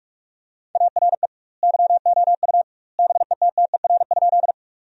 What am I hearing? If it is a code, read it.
Morse code, heard as a keyed tone, about 46 wpm, 700 Hz. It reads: ARE YOU BETTER?
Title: 2x Word-Spacing - Sentences from Top 400 Words - 45wpm